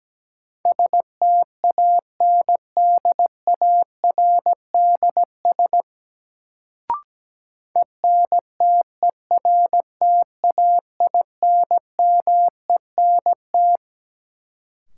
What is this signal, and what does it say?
Morse code, heard as a keyed tone, about 17 wpm, 700 Hz. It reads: STANDARDS ENTERTAINMENT
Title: Sentences from Top 1000 Words - Review Part 6 - New Word Drill - 17wpm